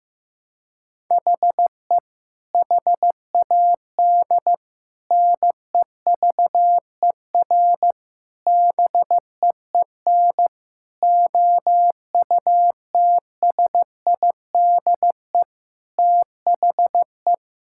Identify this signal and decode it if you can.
Morse code, heard as a keyed tone, about 15 wpm, 700 Hz. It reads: HE HAD NEVER BEEN OUTSIDE THE
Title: Sentences from Top 1600 Words 15wpm